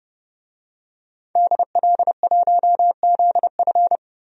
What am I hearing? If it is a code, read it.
Morse code, heard as a keyed tone, about 30 wpm, 700 Hz. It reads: DL1ZF
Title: DXCC-100 Call Signs - 5 Characters - 30wpm